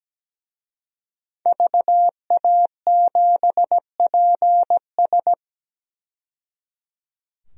Morse code, heard as a keyed tone, about 17 wpm, 700 Hz. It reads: VA7PS
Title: DXCC-100 Call Signs - 5 Characters - 17wpm